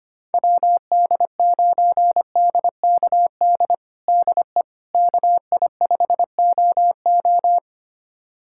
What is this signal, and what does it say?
Morse code, heard as a keyed tone, about 25 wpm, 700 Hz. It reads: WD9DKD DE KI5OO
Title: USA Callsign DE Callsign 25wpm